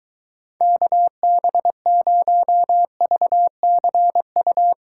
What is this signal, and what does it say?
Morse code, heard as a keyed tone, about 23 wpm, 700 Hz. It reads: KB0VCU